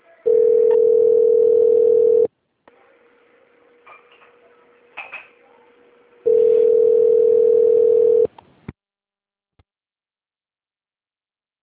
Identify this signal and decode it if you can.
Ringback tone (North America), 440 + 480 Hz.